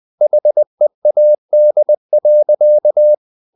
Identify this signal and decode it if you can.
Morse code, heard as a keyed tone, about 20 wpm, 600 Hz. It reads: HEAD.
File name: the old man and the sea part 2 20wpm